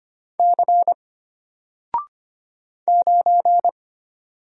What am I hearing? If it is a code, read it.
Morse code, heard as a keyed tone, about 25 wpm, 700 Hz. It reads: C 9
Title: Single Letter-Number 25wpm